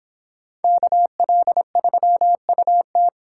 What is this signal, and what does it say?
Morse code, heard as a keyed tone, about 29 wpm, 700 Hz. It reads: KL3UT